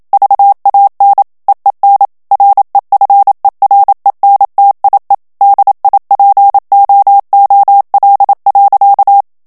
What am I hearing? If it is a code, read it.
Morse code, heard as a keyed tone, about 27 wpm, 800 Hz. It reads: VAN EEN REFERENTIE DIPOOL.
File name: HOLLANDAIS texte clair 1 vitesse 26 mots minute